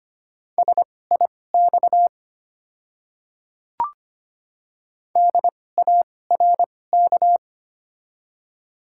Morse code, heard as a keyed tone, about 25 wpm, spaced as for 18 wpm, 700 Hz. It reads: SIX DARK
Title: 2x Character Spacing - Top 500 Words - 25wpm